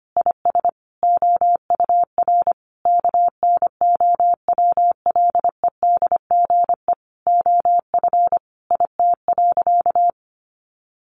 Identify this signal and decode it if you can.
Morse code, heard as a keyed tone, about 25 wpm, 700 Hz. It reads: IS OUR KNOWLEDGE OF IT.
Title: Sentences from Top 1400 Words 25wpm